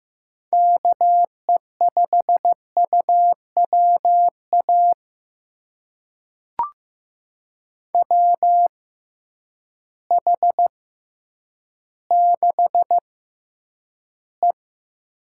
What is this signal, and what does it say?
Morse code, heard as a keyed tone, about 15 wpm, spaced as for 9 wpm, 700 Hz. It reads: KE5UWA WH6E
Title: USA Call Signs - 6 Characters - 6x Character Spacing - 15wpm